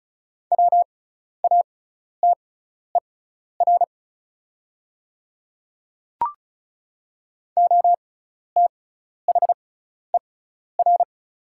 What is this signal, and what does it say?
Morse code, heard as a keyed tone, about 36 wpm, spaced as for 12 wpm, 700 Hz. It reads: WATER OTHER
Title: 6x Character Spacing - Top 100 Words - 35wpm